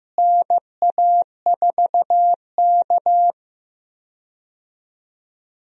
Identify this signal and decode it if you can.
Morse code, heard as a keyed tone, about 15 wpm, 700 Hz. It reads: NA4K
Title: USA Call Signs - 4 Characters - Repeated 2x - 15wpm